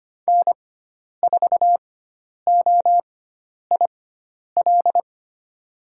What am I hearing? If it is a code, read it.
Morse code, heard as a keyed tone, about 25 wpm, spaced as for 10 wpm, 700 Hz. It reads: N4OIL
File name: Lesson 16g - Letter c - Call Signs 25wpm